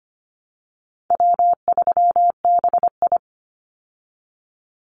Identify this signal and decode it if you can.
Morse code, heard as a keyed tone, about 25 wpm, 700 Hz. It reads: W3BI
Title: Call Signs Repeated 2x - 25wpm